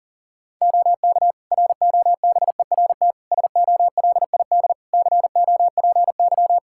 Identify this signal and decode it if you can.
Morse code, heard as a keyed tone, about 40 wpm, 700 Hz. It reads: OK ROBERT SOLID COPY
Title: K1USN SST QSO Practice 40wpm